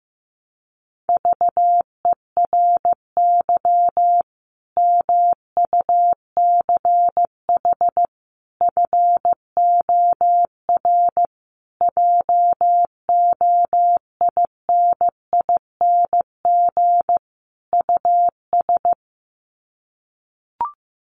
Morse code, heard as a keyed tone, about 15 wpm, 700 Hz. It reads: VERY MUCH FOR JOINING US
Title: Sets of 5 Words 15wpm